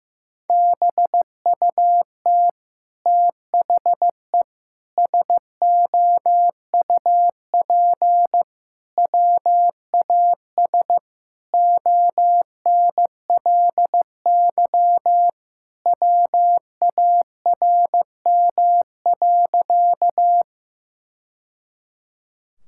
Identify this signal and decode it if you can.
Morse code, heard as a keyed tone, about 15 wpm, 700 Hz. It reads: BUT THE SOUP WAS ONLY WARM.
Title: Sentences from Top 1000 Words - Review Part 6 - 15wpm